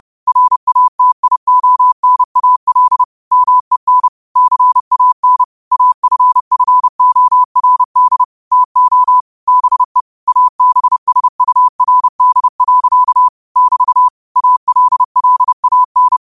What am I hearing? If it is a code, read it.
Morse code, heard as a keyed tone, about 30 wpm, 1000 Hz. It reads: RATIONAL MEN CAN AFFORD TO BE ABSURD. <BT> ALLAN
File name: qotdfiles / 30wpm